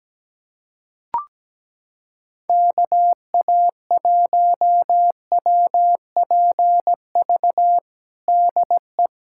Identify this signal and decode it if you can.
Morse code, heard as a keyed tone, about 17 wpm, 700 Hz. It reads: KA1WPV DE
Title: USA Callsign DE Callsign - 6 Characters - 17wpm